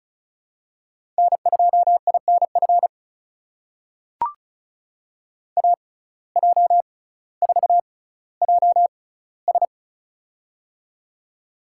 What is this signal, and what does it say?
Morse code, heard as a keyed tone, about 36 wpm, spaced as for 18 wpm, 700 Hz. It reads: N2INF AJ4JS
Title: USA Call Signs - 5 Characters - 6x Character Spacing - 35wpm